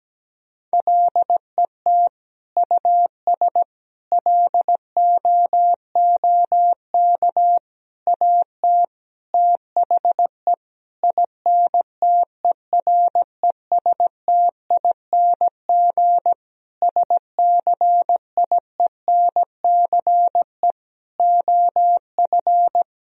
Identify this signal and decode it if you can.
Morse code, heard as a keyed tone, about 17 wpm, 700 Hz. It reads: LET US LOOK AT THE INTERESTING SCIENCE OF
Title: Sentences from Top 1000 Words - Review Part 2 - 17wpm